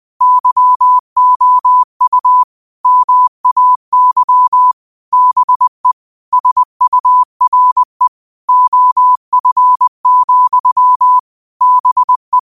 Morse code, heard as a keyed tone, about 20 wpm, 1000 Hz. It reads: YOU MAY BE SURE OF, BE
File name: qotdfiles / 20wpm